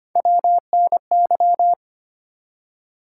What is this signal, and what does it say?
Morse code, heard as a keyed tone, about 25 wpm, 700 Hz. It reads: WNY
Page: morse-code-ninja-practice-25wpm